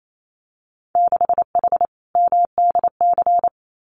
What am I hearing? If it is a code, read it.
Morse code, heard as a keyed tone, about 28 wpm, 700 Hz. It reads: 6H MDC
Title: Winter Field Day Exchanges Encore 28wpm